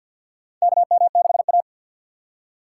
Morse code, heard as a keyed tone, about 47 wpm, 700 Hz. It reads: KM6A
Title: USA Call Signs - 4 Characters - 2x Character Spacing - 50wpm